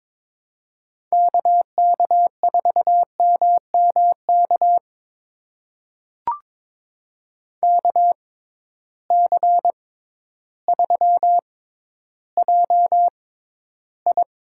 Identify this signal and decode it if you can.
Morse code, heard as a keyed tone, about 22 wpm, spaced as for 12 wpm, 700 Hz. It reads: KK4MMK KC3JI
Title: USA Call Signs - 6 Characters - 6x Character Spacing - 22wpm